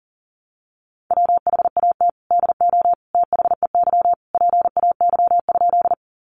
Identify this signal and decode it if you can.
Morse code, heard as a keyed tone, about 40 wpm, 700 Hz. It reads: WHAT DO THEY PAY?
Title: Sentences from Top 700 Words 40wpm